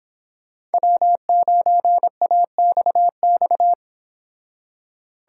Morse code, heard as a keyed tone, about 25 wpm, 700 Hz. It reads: W9AXX